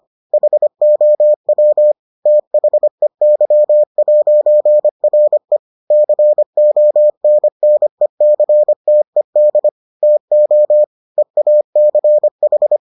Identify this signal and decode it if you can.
Morse code, heard as a keyed tone, about 25 wpm, 600 Hz. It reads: HOW THEY'RE CONNECTED TO EACH